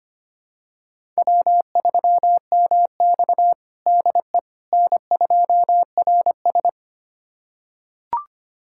Morse code, heard as a keyed tone, about 25 wpm, 700 Hz. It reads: W3MX DE N2RS
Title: USA Callsign DE Callsign - 4 Characters - 25wpm